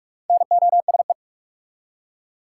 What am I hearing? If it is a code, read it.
Morse code, heard as a keyed tone, about 42 wpm, 700 Hz. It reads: NOSE
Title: Four Letter Words - 45wpm